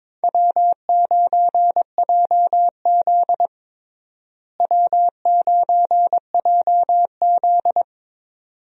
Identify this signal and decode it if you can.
Morse code, heard as a keyed tone, about 22 wpm, 700 Hz. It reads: W9JZ W9JZ
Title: USA Call Signs - 4 Characters - Repeated 2x - 22wpm